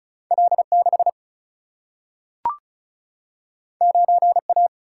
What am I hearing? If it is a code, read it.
Morse code, heard as a keyed tone, about 36 wpm, 700 Hz. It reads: L6 9A
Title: Single-Letter Single-Number 35wpm